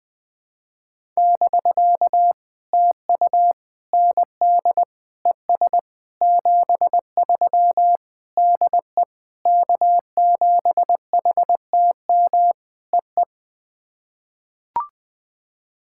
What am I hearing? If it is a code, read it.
Morse code, heard as a keyed tone, about 20 wpm, 700 Hz. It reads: <BK> TU ND ES 73 DE K7HTM EE
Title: POTA QSO Practice - 20wpm